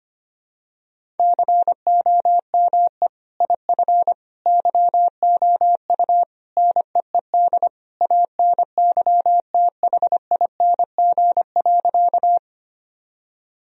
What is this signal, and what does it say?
Morse code, heard as a keyed tone, about 25 wpm, 700 Hz. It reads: COME IF YOU NEED ANYTHING.
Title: Sentences from Top 1000 Words - Review Part 1 - 25wpm